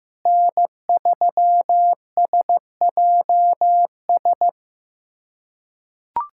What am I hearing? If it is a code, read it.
Morse code, heard as a keyed tone, about 15 wpm, 700 Hz. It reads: N3SJS
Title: USA Callsign DE Callsign 15wpm